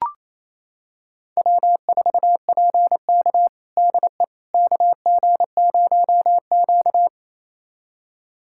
Morse code, heard as a keyed tone, about 28 wpm, 700 Hz. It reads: W4PK DE KG0Q
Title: USA Callsign DE Callsign - 4 Characters - 28wpm